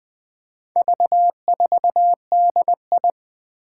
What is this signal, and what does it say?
Morse code, heard as a keyed tone, about 20 wpm, 700 Hz. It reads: V4DI
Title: Four Challenging Letter-Number Combinations 20wpm